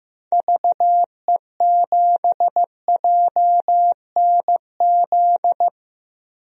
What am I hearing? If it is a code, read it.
Morse code, heard as a keyed tone, about 16 wpm, 700 Hz. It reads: VE7JNZ
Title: DXCC-100 Call Signs - 6 Characters - Repeated 2x - 15wpm